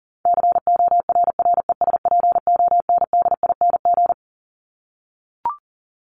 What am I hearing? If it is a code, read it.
Morse code, heard as a keyed tone, about 40 wpm, 700 Hz. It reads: CORRESPONDING
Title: ING Suffix - Top 100 Words with Base Words 40wpm